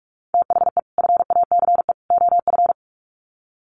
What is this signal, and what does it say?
Morse code, heard as a keyed tone, about 44 wpm, 700 Hz. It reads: THE FACE OF
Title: Sets of 3 Words 45wpm